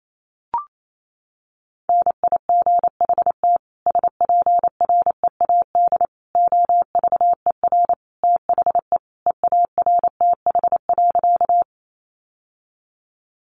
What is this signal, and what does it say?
Morse code, heard as a keyed tone, about 28 wpm, 700 Hz. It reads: NIGHT SPREAD OVER THE EARTH.
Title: Sentences from Top 1200 Words 28wpm